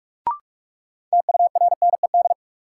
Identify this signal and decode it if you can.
Morse code, heard as a keyed tone, about 47 wpm, 700 Hz. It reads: TURNED
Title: Sets of 4 Words - New Word Drill - 45wpm